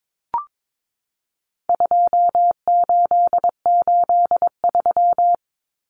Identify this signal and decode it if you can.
Morse code, heard as a keyed tone, about 22 wpm, 700 Hz. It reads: 2883
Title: Common Cut Numbers - 4-Digit 22wpm